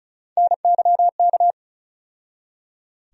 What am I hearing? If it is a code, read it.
Morse code, heard as a keyed tone, about 34 wpm, 700 Hz. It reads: NYK
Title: USA Call Sign Postfixes - 3 Letters 35wpm